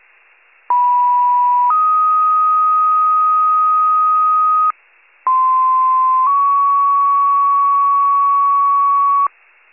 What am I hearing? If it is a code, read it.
Two-tone page, heard as a one-second tone, then a three-second tone, about 980 then 1250 Hz, about 1040 then 1120 Hz. It is heard from 0.7 to 9.3 s.